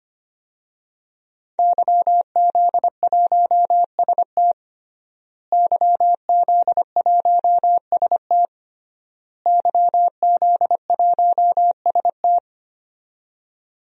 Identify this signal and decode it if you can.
Morse code, heard as a keyed tone, about 25 wpm, 700 Hz. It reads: YZ1ST YZ1ST YZ1ST
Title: DXCC-100 Call Signs - 5 Characters - Repeated 3x - 25wpm